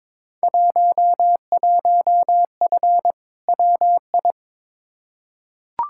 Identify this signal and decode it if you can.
Morse code, heard as a keyed tone, about 22 wpm, 700 Hz. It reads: 11F WI
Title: Field Day Exchanges Encore - 22wpm